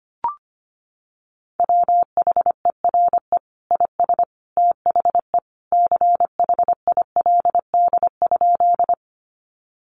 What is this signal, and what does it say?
Morse code, heard as a keyed tone, about 25 wpm, 700 Hz. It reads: WHERE IS THE CHILD?
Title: Sentences from Top 800 Words 25wpm